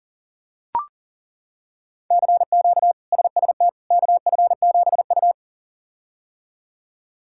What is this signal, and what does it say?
Morse code, heard as a keyed tone, about 37 wpm, 700 Hz. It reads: CQ SST KF7U
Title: K1USN SST QSO Practice - 40wpm